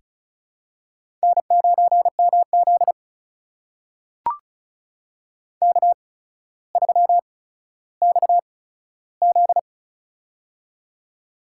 Morse code, heard as a keyed tone, about 36 wpm, spaced as for 15 wpm, 700 Hz. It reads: N9MZ K3XZ
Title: USA Call Signs - 4 Characters - 8x Character Spacing - 35wpm